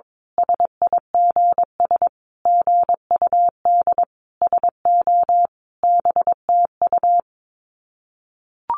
Morse code, heard as a keyed tone, about 22 wpm, 700 Hz. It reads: SIGS GUD SO BTU
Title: Common QSO Phrases 22wpm